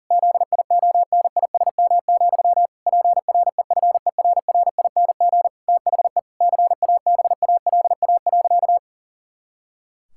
Morse code, heard as a keyed tone, about 40 wpm, 700 Hz. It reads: ZIONISM, PREFERRING THE CABALA.
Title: Pangram Challenge v2 - 40wpm